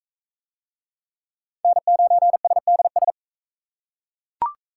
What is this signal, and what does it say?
Morse code, heard as a keyed tone, about 39 wpm, 700 Hz. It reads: N9SDS